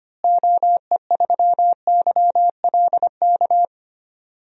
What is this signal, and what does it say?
Morse code, heard as a keyed tone, about 25 wpm, 700 Hz. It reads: OE3YLK
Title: DXCC-100 Call Signs - 6 Characters - Repeated 3x - 25wpm